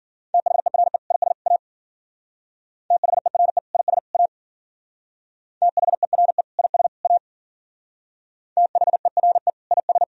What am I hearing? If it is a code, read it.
Morse code, heard as a keyed tone, about 50 wpm, 700 Hz. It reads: THERE IS A THERE IS A THERE IS A THERE IS